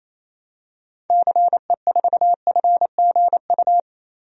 Morse code, heard as a keyed tone, about 28 wpm, 700 Hz. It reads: CE4FGU
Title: DXCC-100 Call Signs - 6 Characters - 28wpm